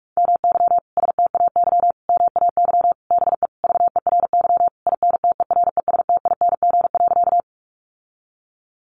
Morse code, heard as a keyed tone, about 44 wpm, 700 Hz. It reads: MY STAY MAY BE VERY INTERESTING.
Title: Sentences from Top 1000 Words 45wpm